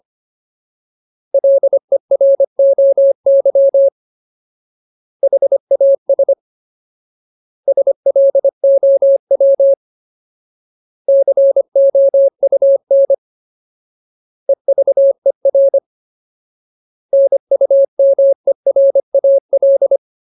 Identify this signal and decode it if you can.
Morse code, heard as a keyed tone, about 25 wpm, 550 Hz. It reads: LEROY HAS SLOW COUN EVER NUMERAL
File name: Ultimate Word List / 25WPM Ultimate